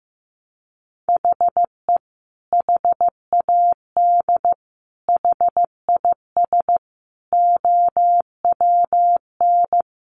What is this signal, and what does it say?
Morse code, heard as a keyed tone, about 15 wpm, 700 Hz. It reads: HE HAD HIS OWN
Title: Sentences from Top 1200 Words 15wpm